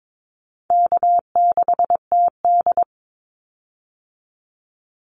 Morse code, heard as a keyed tone, about 22 wpm, 700 Hz. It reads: K6TD
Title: Call Signs Repeated 3x - 22wpm